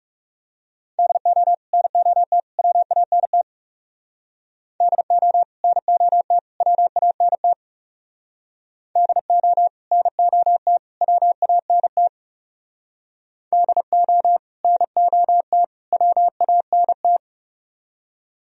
Morse code, heard as a keyed tone, about 38 wpm, 700 Hz. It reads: DO NOT WANT DO NOT WANT DO NOT WANT DO NOT WANT